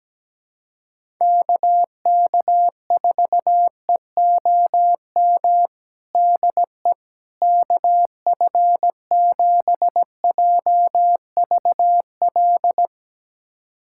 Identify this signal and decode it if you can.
Morse code, heard as a keyed tone, about 17 wpm, 700 Hz. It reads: KK4EOM DE KF7JVL
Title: USA Callsign DE Callsign - 6 Characters - 17wpm